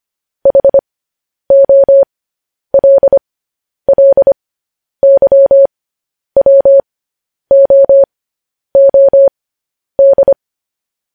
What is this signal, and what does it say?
Morse code, heard as a keyed tone, about 25 wpm, spaced as for 10 wpm, 550 Hz. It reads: HOLLYWOOD
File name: podcast / News headlines-1x-25x10